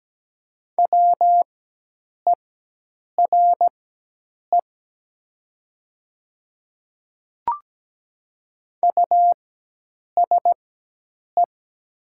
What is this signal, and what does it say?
Morse code, heard as a keyed tone, about 17 wpm, spaced as for 8 wpm, 700 Hz. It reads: WERE USE
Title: 4x Character Spacing - Top 100 Words - 17wpm